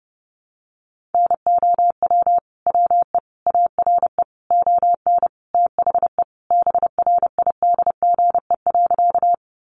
Morse code, heard as a keyed tone, about 30 wpm, 700 Hz. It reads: NOW WE ARE ON THE BRIDGE.
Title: Sentences from Top 1200 Words 30wpm